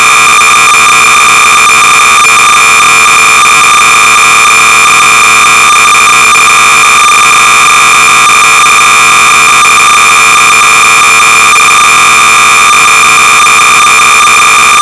Cassette data (Kansas City type).
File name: Super-Star-Trek (V2.1)-(Sorcerers User Group-Modified by G.Lister)-BASIC-Sorcerer-300baud